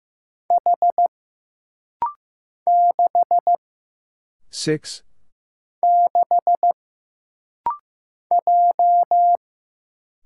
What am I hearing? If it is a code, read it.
Morse code, heard as a keyed tone, about 15 wpm, 700 Hz. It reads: H 6 6 J
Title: Single Letter-Number Alt - Rapid-Fire 15wpm